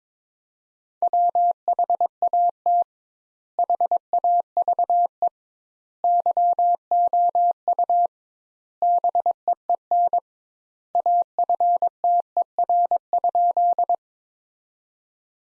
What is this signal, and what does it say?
Morse code, heard as a keyed tone, about 22 wpm, 700 Hz. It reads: WHAT HAVE YOU BEEN AFTER?
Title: Sentences Top 200 Words - 22wpm